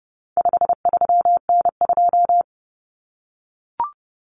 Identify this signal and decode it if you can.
Morse code, heard as a keyed tone, about 30 wpm, 700 Hz. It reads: 53N2
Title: Common Cut Numbers - 4-Digit 30wpm